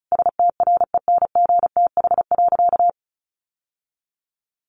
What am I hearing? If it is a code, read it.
Morse code, heard as a keyed tone, about 34 wpm, 700 Hz. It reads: STRENGTH.
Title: Sentences from Top 1400 Words 35wpm